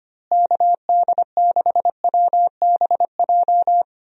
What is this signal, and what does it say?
Morse code, heard as a keyed tone, about 25 wpm, 700 Hz. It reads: KD6WBJ
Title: USA Call Signs - 6 Characters - Repeated 3x - 25wpm